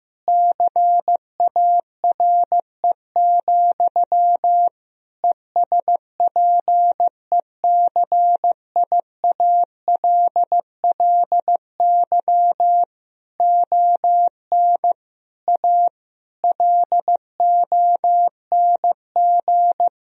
Morse code, heard as a keyed tone, about 16 wpm, 700 Hz. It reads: CARE, ESPECIALLY ON A LONG
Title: Sentences from Top 1000 Words - Review Part 4 - 15wpm